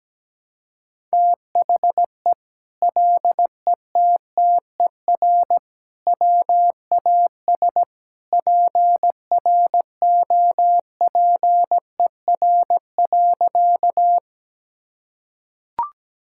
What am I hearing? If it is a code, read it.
Morse code, heard as a keyed tone, about 17 wpm, 700 Hz. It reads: THE LETTER WAS PROPER.
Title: Sentences from Top 1000 Words - Review Part 5 - 17wpm